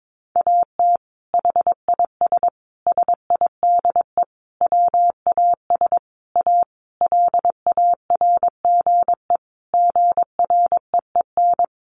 Morse code, heard as a keyed tone, about 22 wpm, 700 Hz. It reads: AT HIS SIDE WAS A LARGE GREEN
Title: Sentences from Top 500 Words 22wpm